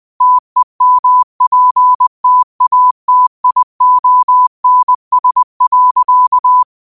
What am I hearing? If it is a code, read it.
Morse code, heard as a keyed tone, about 20 wpm, 1000 Hz. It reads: TEMPTATIONS.
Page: New quotes every day in morse code at 20 Words per minute.